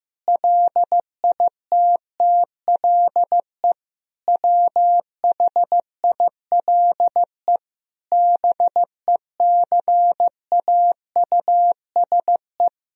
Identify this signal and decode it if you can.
Morse code, heard as a keyed tone, about 15 wpm, 700 Hz. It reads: LITTLE WHILE BECAUSE
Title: Sentences from Top 1000 Words - Review Part 3 - 15wpm